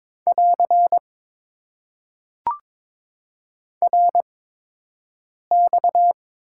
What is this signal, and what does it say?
Morse code, heard as a keyed tone, about 22 wpm, spaced as for 6 wpm, 700 Hz. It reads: <AR> RX
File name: Top 90 QSO Elements - 8x Character Spacing - 22wpm